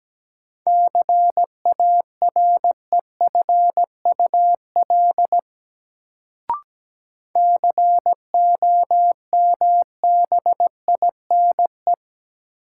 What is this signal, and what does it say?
Morse code, heard as a keyed tone, about 17 wpm, 700 Hz. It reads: CAREFUL COMBINE
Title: Sentences from Top 1600 Words - New Word Drill - 17wpm